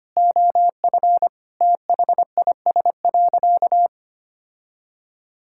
Morse code, heard as a keyed tone, about 25 wpm, 700 Hz. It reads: OF THIS.
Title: Sentences from Top 1000 Words - Review Part 6 - 25wpm